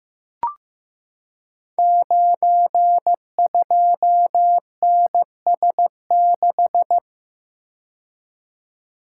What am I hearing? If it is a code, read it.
Morse code, heard as a keyed tone, about 15 wpm, 700 Hz. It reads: 92NS6
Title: Code Groups - 5 Letter:Numbers - 15wpm